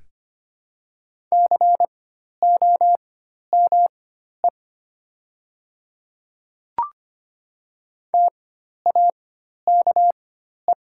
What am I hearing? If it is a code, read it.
Morse code, heard as a keyed tone, about 25 wpm, spaced as for 12 wpm, 700 Hz. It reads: COME TAKE
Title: 4x Character Spacing - Top 300 Words - 25wpm